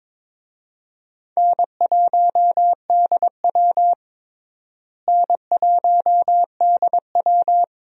Morse code, heard as a keyed tone, about 22 wpm, 700 Hz. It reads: N1DW N1DW
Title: USA Call Signs - 4 Characters - Repeated 2x - 22wpm